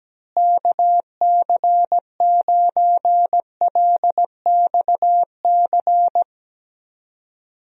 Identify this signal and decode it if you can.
Morse code, heard as a keyed tone, about 17 wpm, 700 Hz. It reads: KC9LXC
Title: USA Call Signs - 6 Characters - Repeated 2x - 17wpm